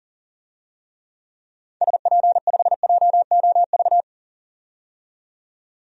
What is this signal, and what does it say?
Morse code, heard as a keyed tone, about 40 wpm, 700 Hz. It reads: SP5JOV
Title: DXCC-100 Call Signs - 6 Characters - 40wpm